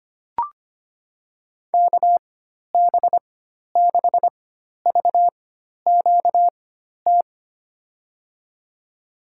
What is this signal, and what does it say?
Morse code, heard as a keyed tone, about 25 wpm, spaced as for 12 wpm, 700 Hz. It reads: KB6VQT
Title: USA Call Signs - 6 Characters - 4x Character Spacing - 25wpm